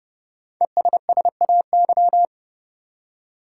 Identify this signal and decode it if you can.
Morse code, heard as a keyed tone, about 30 wpm, 700 Hz. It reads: ESSAY
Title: Five Letter Words - 30wpm